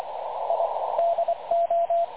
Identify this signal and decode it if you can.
Morse code, heard as a keyed tone, about 25 wpm, 680 Hz, with radio noise. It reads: I DO
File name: dlazaawansowanych / nagraniazpasma / websdr recording 3567.0kHz